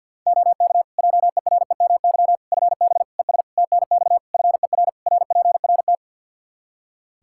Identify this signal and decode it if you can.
Morse code, heard as a keyed tone, about 49 wpm, 700 Hz. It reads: OK JEREMY FB ES TNX FER RPRT
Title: Practice QSO - Exchange 2 - Info - 50wpm